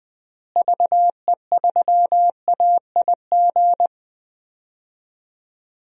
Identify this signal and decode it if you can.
Morse code, heard as a keyed tone, about 21 wpm, 700 Hz. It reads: VE3AIG
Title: DXCC-100 Call Signs - 6 Characters - Repeated 3x - 20wpm